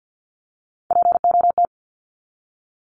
Morse code, heard as a keyed tone, about 48 wpm, 700 Hz. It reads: POT
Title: Top 751-1000 Words 50wpm